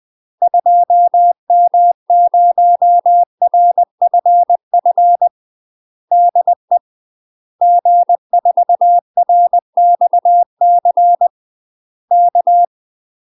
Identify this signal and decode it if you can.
Morse code, heard as a keyed tone, about 20 wpm, 700 Hz. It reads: 2M0RFF DE G4RXC K
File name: UK MASTER QSOs TRAINING / 20WPM / UK MASTER QSOs 0011